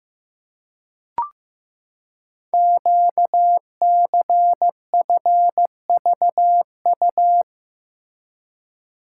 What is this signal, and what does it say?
Morse code, heard as a keyed tone, about 15 wpm, 700 Hz. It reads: QCFVU
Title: Code Groups - 5 Letters - 15wpm